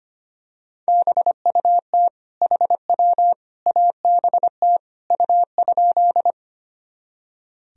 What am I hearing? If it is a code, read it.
Morse code, heard as a keyed tone, about 25 wpm, 700 Hz. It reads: BUT HW ABT U?
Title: Sentences from Top 100 Words and Abbreviations 25wpm